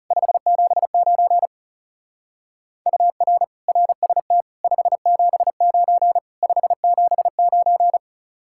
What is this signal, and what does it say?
Morse code, heard as a keyed tone, about 36 wpm, 700 Hz. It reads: 579 UR RST 579 579
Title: Speed-Racing Common QSO Phrases - 30wpm